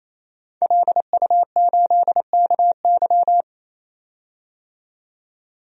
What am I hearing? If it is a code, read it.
Morse code, heard as a keyed tone, about 28 wpm, 700 Hz. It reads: LU8KY
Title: DXCC-100 Call Signs - 5 Characters - 28wpm